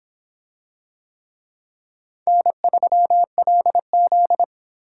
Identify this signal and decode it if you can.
Morse code, heard as a keyed tone, about 26 wpm, 700 Hz. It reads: N3LZ